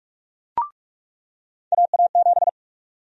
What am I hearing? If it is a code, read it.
Morse code, heard as a keyed tone, about 46 wpm, 700 Hz. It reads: AA7
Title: USA Call Sign Prefixes - 45wpm